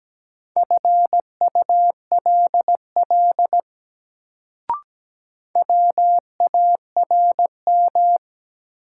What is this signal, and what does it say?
Morse code, heard as a keyed tone, about 17 wpm, 700 Hz. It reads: FULL WARM
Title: Sentences from Top 500 Words - New Word Drill - 17wpm